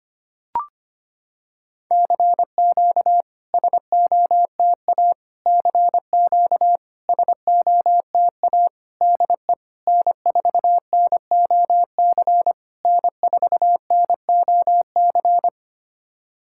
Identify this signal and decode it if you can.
Morse code, heard as a keyed tone, about 25 wpm, 700 Hz. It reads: CQ SOTA CQ SOTA DE N4NOC N4NOC
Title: SOTA QSO Practice 25wpm